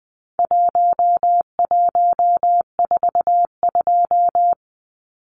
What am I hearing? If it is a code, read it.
Morse code, heard as a keyed tone, about 20 wpm, 700 Hz. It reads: 1142
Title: Numbers - Contest Style - 4-Digit 20wpm